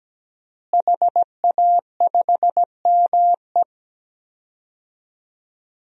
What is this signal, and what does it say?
Morse code, heard as a keyed tone, about 17 wpm, 700 Hz. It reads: HA5ME
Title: DXCC-100 Call Signs - 5 Characters - Repeated 3x - 17wpm